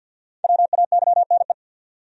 Morse code, heard as a keyed tone, about 50 wpm, 700 Hz. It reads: WAYNE
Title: USA Common Names - Max 5 Letters - 50wpm